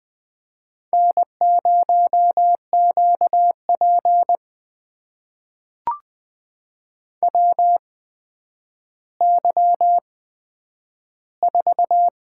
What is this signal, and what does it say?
Morse code, heard as a keyed tone, about 20 wpm, spaced as for 10 wpm, 700 Hz. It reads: N0QP WY4
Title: USA Call Signs - 4 Characters - 8x Character Spacing - 20wpm